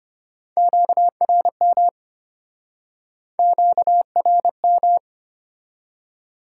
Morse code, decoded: QRM QRM

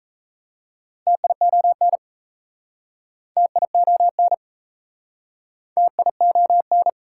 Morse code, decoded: TION TION TION

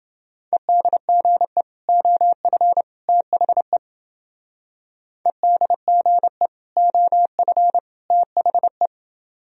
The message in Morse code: EDGE OF THE EDGE OF THE

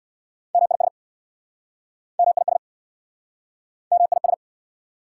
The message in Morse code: MIS MIS MIS

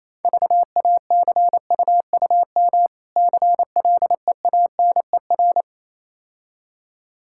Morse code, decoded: VACUUM CLEANER

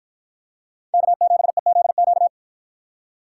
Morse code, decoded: K7EZQ

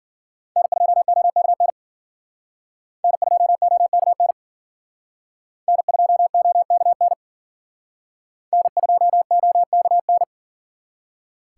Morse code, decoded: N2OKN N2OKN N2OKN N2OKN

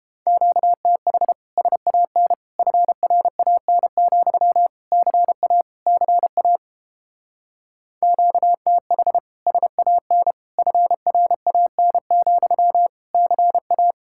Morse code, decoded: QTH SAN FRAN, CA CA QTH SAN FRAN, CA

